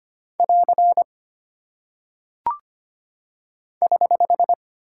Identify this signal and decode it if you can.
Morse code, heard as a keyed tone, about 25 wpm, 700 Hz. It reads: <AR> <HH>
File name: Single Punctuation-Prosign - 25wpm